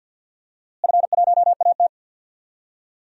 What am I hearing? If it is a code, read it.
Morse code, heard as a keyed tone, about 47 wpm, 700 Hz. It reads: F1AT
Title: DXCC-100 Call Signs - 4 Characters - Repeated 3x - 50wpm